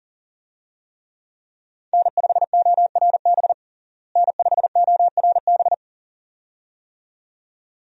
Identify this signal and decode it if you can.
Morse code, heard as a keyed tone, about 37 wpm, 700 Hz. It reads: N5ORB N5ORB
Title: USA Call Signs - 5 Characters - Repeated 2x - 40wpm